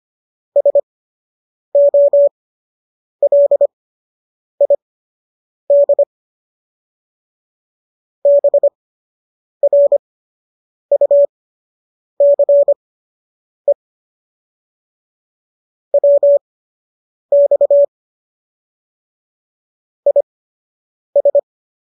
Morse code, heard as a keyed tone, about 25 wpm, spaced as for 8 wpm, 570 Hz. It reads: SOLID BRUCE WX IS